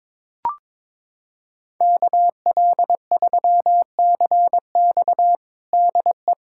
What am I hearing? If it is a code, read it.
Morse code, heard as a keyed tone, about 22 wpm, 700 Hz. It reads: KL3CX DE
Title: USA Callsign DE Callsign 22wpm